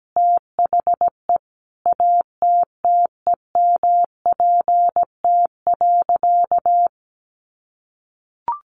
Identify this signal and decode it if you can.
Morse code, heard as a keyed tone, about 17 wpm, 700 Hz. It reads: THE ATTEMPT.
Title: Sentences from Top 1000 Words 17wpm